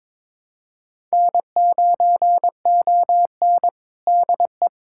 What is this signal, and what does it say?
Morse code, heard as a keyed tone, about 22 wpm, 700 Hz. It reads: N9ON DE
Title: USA Callsign DE Callsign - 4 Characters - 22wpm